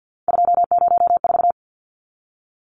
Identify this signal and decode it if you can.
Morse code, heard as a keyed tone, about 50 wpm, 700 Hz. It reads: 204
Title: Numbers - 3-Digit 50wpm